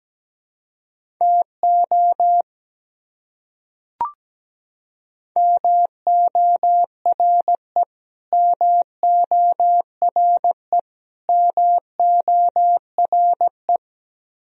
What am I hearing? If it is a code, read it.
Morse code, heard as a keyed tone, about 17 wpm, 700 Hz. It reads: TO MORE MORE MORE
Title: Top 100 Words Repeated 17wpm